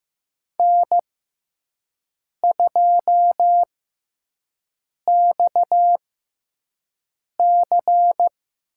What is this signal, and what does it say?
Morse code, heard as a keyed tone, about 15 wpm, spaced as for 5 wpm, 700 Hz. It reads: N2XC
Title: USA Call Signs - 6x Character Spacing - 15wpm